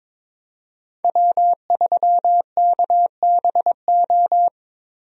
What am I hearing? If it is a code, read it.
Morse code, heard as a keyed tone, about 22 wpm, 700 Hz. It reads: W3KBO
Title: USA Call Signs - 5 Characters - Repeated 3x - 22wpm